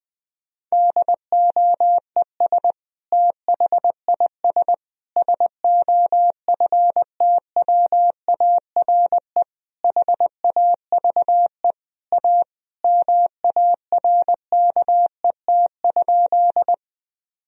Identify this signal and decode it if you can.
Morse code, heard as a keyed tone, about 21 wpm, 700 Hz. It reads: DOES THIS SOFTWARE HAVE A MARKET?
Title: Sentences from Top 1000 Words - Review Part 3 - 20wpm